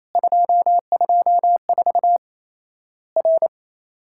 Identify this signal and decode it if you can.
Morse code, heard as a keyed tone, about 27 wpm, 700 Hz. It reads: 224 R
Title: Worked All Europe DX Contest - QTC Exchange - 28wpm